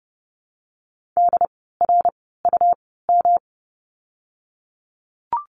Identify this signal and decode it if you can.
Morse code, heard as a keyed tone, about 30 wpm, spaced as for 17 wpm, 700 Hz. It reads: DRUM
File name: course / 30:10wpm / 108 Lesson 18f - Letter m - Words 30wpm